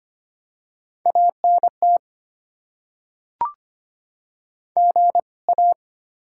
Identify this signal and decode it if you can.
Morse code, heard as a keyed tone, about 25 wpm, 700 Hz. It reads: ANT GA
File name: Top 20 QSO Elements - 2x Character Spacing - 25wpm